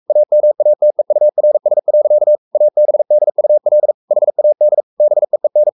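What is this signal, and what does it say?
Morse code, heard as a keyed tone, about 44 wpm, 600 Hz. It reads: AMATEURS. ABDUL HAD BEEN
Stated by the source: Morse code at 43 wpm sending Ken Follett's novel Never in English, from CW med Gnister.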